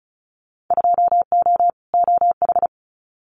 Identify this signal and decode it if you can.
Morse code, heard as a keyed tone, about 36 wpm, 690 Hz. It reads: 2O OH
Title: Winter Field Day Exchanges 35wpm